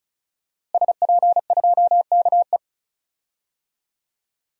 Morse code, heard as a keyed tone, about 36 wpm, 700 Hz. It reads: SP2KE
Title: DXCC-100 Call Signs - 5 Characters - Repeated 3x - 35wpm